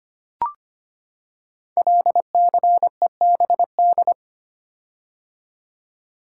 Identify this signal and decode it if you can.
Morse code, heard as a keyed tone, about 25 wpm, 700 Hz. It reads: LCEBD
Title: Code Groups - 5 Letters - 25wpm